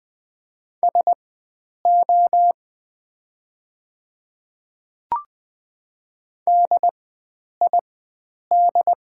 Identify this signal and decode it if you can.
Morse code, heard as a keyed tone, about 20 wpm, spaced as for 9 wpm, 700 Hz. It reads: SO DID
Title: 4x Character Spacing - Top 300 Words - 20wpm